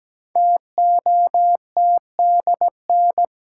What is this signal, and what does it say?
Morse code, heard as a keyed tone, about 17 wpm, 700 Hz. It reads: TOTDN
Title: Code Groups - 5 Letters - 17wpm